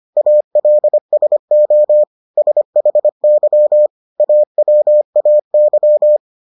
Morse code, heard as a keyed tone, about 25 wpm, 600 Hz. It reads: ALSO SHY AWAY